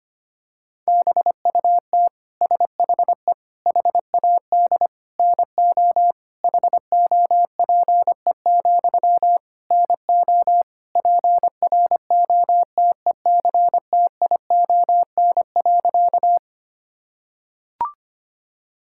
Morse code, heard as a keyed tone, about 25 wpm, 700 Hz. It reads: BUT SHE HAD NO HOPE, NO PROTECTION.
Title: Sentences from Top 1600 Words 25wpm